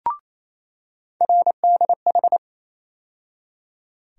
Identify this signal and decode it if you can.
Morse code, heard as a keyed tone, about 28 wpm, 700 Hz. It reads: RDH
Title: USA Call Sign Postfixes - 3 Letters 28wpm